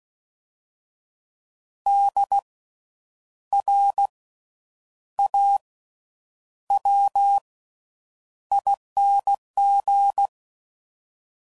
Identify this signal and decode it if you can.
Morse code, heard as a keyed tone, about 16 wpm, spaced as for 8 wpm, 800 Hz. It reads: DRAWING
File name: Suffixes / Special INGs / Slow With Faster Ings